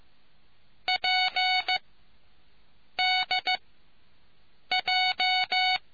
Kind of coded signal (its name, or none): Morse code